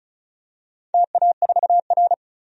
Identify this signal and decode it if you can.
Morse code, heard as a keyed tone, about 36 wpm, 700 Hz. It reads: TA4R
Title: DXCC-100 Call Signs - 4 Characters - 35wpm